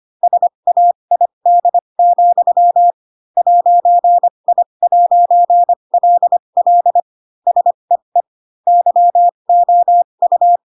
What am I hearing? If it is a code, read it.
Morse code, heard as a keyed tone, about 25 wpm, spaced as for 22 wpm, 700 Hz. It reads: SAID, 'I'LL SEE YOU
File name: Pinocchio / Pinocchio ch7 18 25 ik1hge